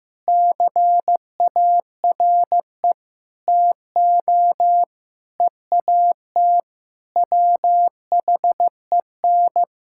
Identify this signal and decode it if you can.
Morse code, heard as a keyed tone, about 15 wpm, 700 Hz. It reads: CARE TO EAT WHEN
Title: Sentences from Top 1600 Words 15wpm